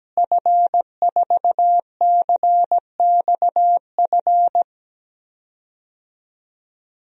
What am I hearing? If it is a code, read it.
Morse code, heard as a keyed tone, about 17 wpm, 700 Hz. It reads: F4CXF
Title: DXCC-100 Call Signs - 5 Characters - Repeated 3x - 17wpm